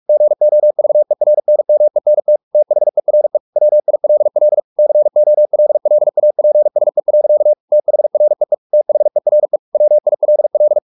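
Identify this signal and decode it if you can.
Morse code, heard as a keyed tone, about 44 wpm, 600 Hz. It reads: GOVERNMENT THERE WILL COLLAPSE. THREE THERE WILL
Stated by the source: Morse practice at 45 wpm, the novel Never by Ken Follett in English, from CW med Gnister.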